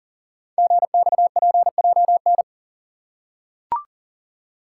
Morse code, heard as a keyed tone, about 42 wpm, 690 Hz. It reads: GXPJN
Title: Code Groups - 5 Letter:Numbers - 40wpm